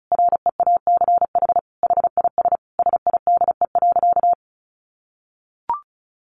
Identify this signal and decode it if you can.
Morse code, heard as a keyed tone, about 34 wpm, 700 Hz. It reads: REACH HIS SIDE.
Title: Sentences from Top 400 Words 35wpm